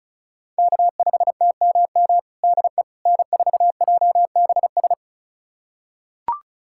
Morse code, heard as a keyed tone, about 36 wpm, 700 Hz. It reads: K5TMM DE N4JBS
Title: USA Callsign DE Callsign - 5 Characters - 35wpm